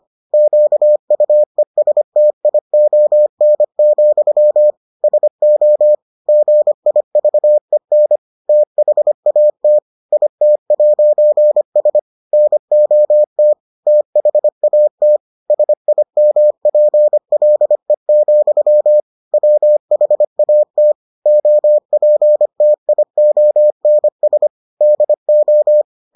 Morse code, heard as a keyed tone, about 25 wpm, 600 Hz. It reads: QUESTION, SO GIVEN THAT IT'S NOT THAT SIMPLE, WHAT OPTIONS DO